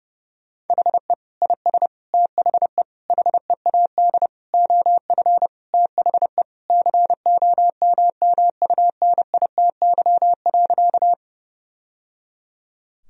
Morse code, heard as a keyed tone, about 30 wpm, 700 Hz. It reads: HE IS THE HEAD OF THE COMMUNITY.
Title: Sentences from Top 1000 Words - Review Part 4 - 30wpm